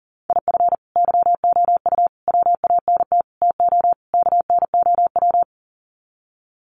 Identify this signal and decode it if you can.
Morse code, heard as a keyed tone, about 40 wpm, 700 Hz. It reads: IF YOU WANT TO KNOW
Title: Sets of 5 Words Encore2 40wpm